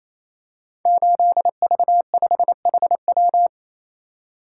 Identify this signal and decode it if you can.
Morse code, heard as a keyed tone, about 27 wpm, 700 Hz. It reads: 8V5HW
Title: Code Groups - 5 Letter:Numbers - 28wpm